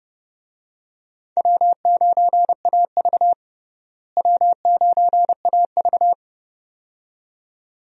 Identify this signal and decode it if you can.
Morse code, heard as a keyed tone, about 32 wpm, 700 Hz. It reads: W9AV W9AV
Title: USA Call Signs - 4 Characters - Repeated 2x - 30wpm